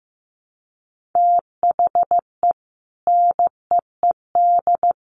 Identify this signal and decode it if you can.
Morse code, heard as a keyed tone, about 15 wpm, 700 Hz. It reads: THE NEED
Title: Sets of 3 Words 15wpm